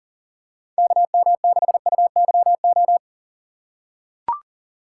Morse code, heard as a keyed tone, about 40 wpm, 700 Hz. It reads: KM6UYO
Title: USA Call Signs - 6 Characters - 4x Character Spacing - 40wpm